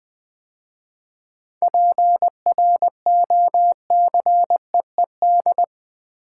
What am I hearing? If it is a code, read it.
Morse code, heard as a keyed tone, about 20 wpm, 700 Hz. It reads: PROCEED
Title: 7 Letter Words - First 3 Letters Given - 20wpm